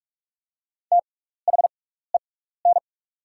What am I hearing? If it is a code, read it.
Morse code, heard as a keyed tone, about 44 wpm, spaced as for 15 wpm, 700 Hz. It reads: THEN
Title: 6x Character Spacing - Top 100 Words - 45wpm